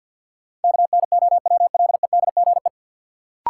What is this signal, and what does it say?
Morse code, heard as a keyed tone, about 47 wpm, 700 Hz. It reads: KNOWLEDGE
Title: Sentences from Top 600 Words - New Word Drill - 50wpm